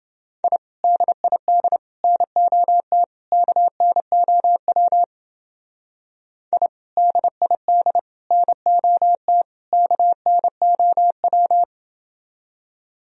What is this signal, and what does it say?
Morse code, heard as a keyed tone, about 29 wpm, 700 Hz. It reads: I DID NOT KNOW I DID NOT KNOW